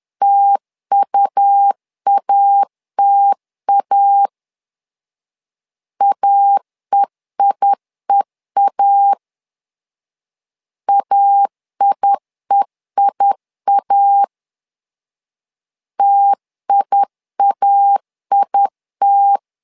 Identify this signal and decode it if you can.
Morse code, heard as a keyed tone, about 10 wpm, 800 Hz. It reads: TUATA AEIEA AIEIA TIAIT
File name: LEZIONI / 2 UHDN / 2 U 10wpm 1622